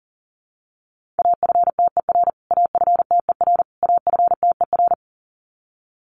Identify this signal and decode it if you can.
Morse code, heard as a keyed tone, about 40 wpm, 700 Hz. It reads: AFTER AFTER AFTER
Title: Top 500 Words Repeated 40wpm